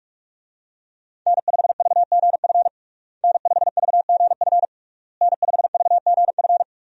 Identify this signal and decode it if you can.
Morse code, heard as a keyed tone, about 47 wpm, 700 Hz. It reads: N5VGF N5VGF N5VGF
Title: USA Call Signs - 5 Characters - Repeated 3x - 45wpm